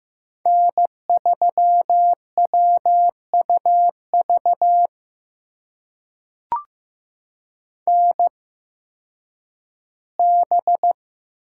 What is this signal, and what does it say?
Morse code, heard as a keyed tone, about 15 wpm, spaced as for 10 wpm, 700 Hz. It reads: N3WUV NB
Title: USA Call Signs - 8x Character Spacing - 15wpm